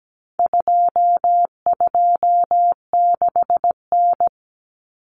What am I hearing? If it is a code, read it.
Morse code, heard as a keyed tone, about 17 wpm, 700 Hz. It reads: 226N
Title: Common Cut Numbers - 4-Digit 17wpm